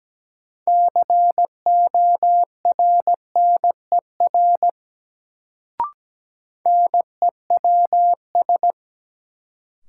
Morse code, heard as a keyed tone, about 17 wpm, 700 Hz. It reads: CORNER NEWS
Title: Sets of 5 Words - New Word Drill - 17wpm